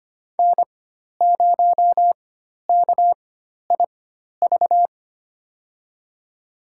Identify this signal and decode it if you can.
Morse code, heard as a keyed tone, about 25 wpm, spaced as for 12 wpm, 700 Hz. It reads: N0KIV
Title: USA Call Signs - 5 Characters - 4x Character Spacing - 25wpm